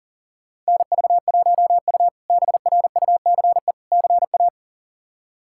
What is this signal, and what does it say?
Morse code, heard as a keyed tone, about 40 wpm, 700 Hz. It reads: NV1U BRUCE CA
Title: K1USN SST QSO Practice - 40wpm